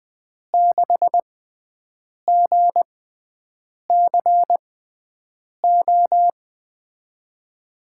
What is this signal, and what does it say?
Morse code, heard as a keyed tone, about 21 wpm, spaced as for 7 wpm, 700 Hz. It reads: 6GCO
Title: USA Call Signs - 6x Character Spacing - 20wpm